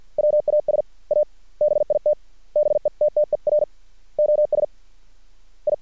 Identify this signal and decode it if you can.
Morse code, heard as a keyed tone, about 50 wpm, 600 Hz. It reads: WAS A BIT BETTER OS I
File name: archived / CRSnet